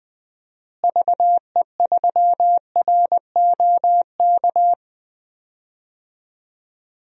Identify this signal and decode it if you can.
Morse code, heard as a keyed tone, about 20 wpm, 700 Hz. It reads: VE3ROK
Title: DXCC-100 Call Signs - 6 Characters - Repeated 2x - 20wpm